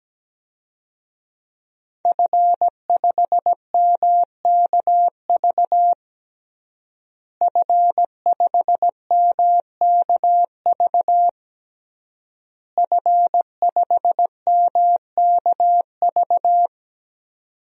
Morse code, heard as a keyed tone, about 17 wpm, 700 Hz. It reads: F5MKV F5MKV F5MKV
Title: DXCC-100 Call Signs - 5 Characters - Repeated 3x - 17wpm